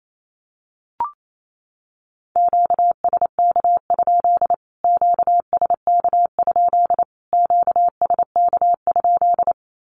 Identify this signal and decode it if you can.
Morse code, heard as a keyed tone, about 28 wpm, 700 Hz. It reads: QSK? QSK? QSK?
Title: Q-Codes Repeated 3x 28wpm